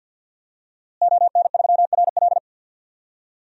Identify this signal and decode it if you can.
Morse code, heard as a keyed tone, about 52 wpm, 700 Hz. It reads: ON3RL
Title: DXCC-100 Call Signs - 5 Characters - Repeated 3x - 50wpm